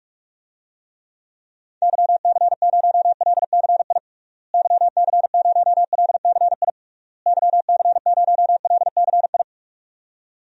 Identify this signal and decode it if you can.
Morse code, heard as a keyed tone, about 46 wpm, 700 Hz. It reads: YC0LCI YC0LCI YC0LCI
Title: DXCC-100 Call Signs - 6 Characters - Repeated 3x - 45wpm